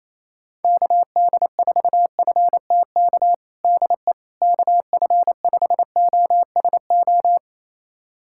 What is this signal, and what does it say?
Morse code, heard as a keyed tone, about 28 wpm, 700 Hz. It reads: KD4FTK DE KF5OSO
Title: USA Callsign DE Callsign 28wpm